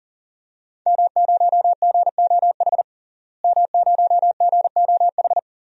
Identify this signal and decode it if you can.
Morse code, heard as a keyed tone, about 40 wpm, 700 Hz. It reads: M0GOH M0GOH
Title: DXCC-100 Call Signs - 5 Characters - Repeated 2x - 40wpm